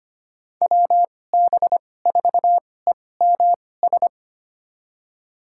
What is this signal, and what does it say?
Morse code, heard as a keyed tone, about 25 wpm, spaced as for 18 wpm, 700 Hz. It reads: WB4EMS
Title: USA Call Signs - 2x Character Spacing - 25wpm